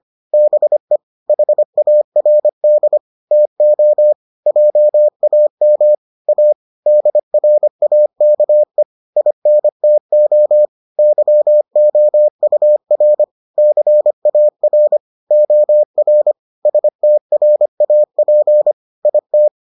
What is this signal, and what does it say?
Morse code, heard as a keyed tone, about 25 wpm, 600 Hz. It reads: BE HARD TO JAM A DRAKE INTO YOUR CAR OR STRAP IT